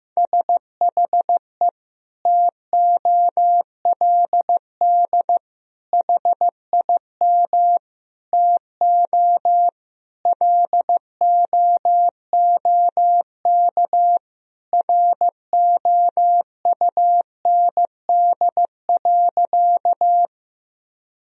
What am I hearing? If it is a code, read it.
Morse code, heard as a keyed tone, about 15 wpm, 700 Hz. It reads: SHE TOLD HIM TO LOOK ROUND.
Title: Sentences from Top 1000 Words - Review Part 1 - 15wpm